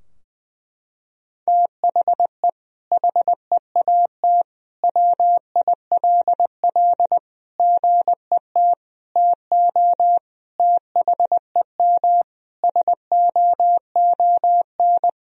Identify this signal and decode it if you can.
Morse code, heard as a keyed tone, about 19 wpm, 700 Hz. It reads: THE HEAT WILL GET TO THEM SOON
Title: Sentences from Top 1000 Words - Review Part 3 - 20wpm